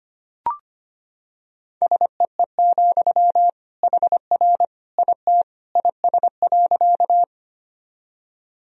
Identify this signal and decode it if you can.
Morse code, heard as a keyed tone, about 25 wpm, 700 Hz. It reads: SEE, HR IT IS.
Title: Sentences from Top 200 Words and Abbreviations 25wpm